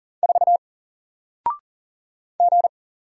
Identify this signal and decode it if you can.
Morse code, heard as a keyed tone, about 41 wpm, 700 Hz. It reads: 4 G
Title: Single Character (Letter:Number:Punctuation:Prosign) 40wpm